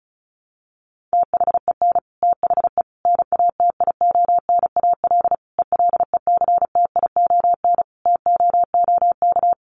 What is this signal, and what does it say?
Morse code, heard as a keyed tone, about 34 wpm, 700 Hz. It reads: THEN THE NATIONAL ELECTION TOOK
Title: Sentences from Top 1500 Words 35wpm